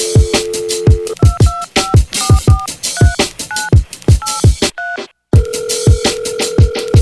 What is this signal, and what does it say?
DTMF, heard as telephone touch tones, tones of about 200 ms. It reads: 225443983